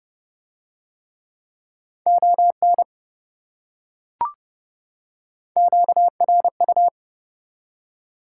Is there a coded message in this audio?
Yes: Morse code from 2.1 to 6.9 s, heard as a keyed tone, about 27 wpm, 700 Hz. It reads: ON QRU